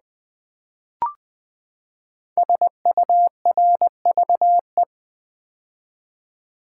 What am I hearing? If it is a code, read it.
Morse code, heard as a keyed tone, about 20 wpm, 700 Hz. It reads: SURVE
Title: Build A Word - 5 Letters At A Time - 12-Letter Words - 20wpm